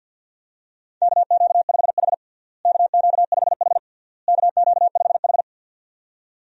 Morse code, heard as a keyed tone, about 51 wpm, 700 Hz. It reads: KQ5H KQ5H KQ5H
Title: USA Call Signs - 4 Characters - Repeated 3x - 50wpm